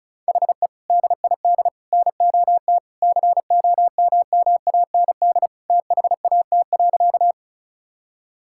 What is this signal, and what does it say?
Morse code, heard as a keyed tone, about 36 wpm, 700 Hz. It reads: HE DID NOT COMMAND THAT.
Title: Sentences from Top 1600 Words 35wpm